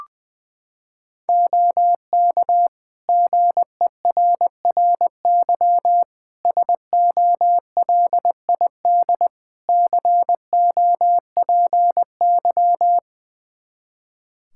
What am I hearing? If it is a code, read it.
Morse code, heard as a keyed tone, about 21 wpm, 700 Hz. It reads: OK GERRY SOLID COPY
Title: Practice QSO - Exchange 3 - More Info - 20wpm